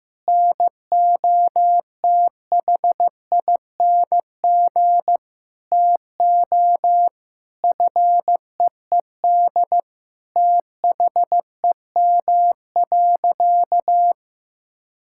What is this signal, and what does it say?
Morse code, heard as a keyed tone, about 16 wpm, 700 Hz. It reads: NOTHING TO FEED THEM.
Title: Sentences from Top 1600 Words 15wpm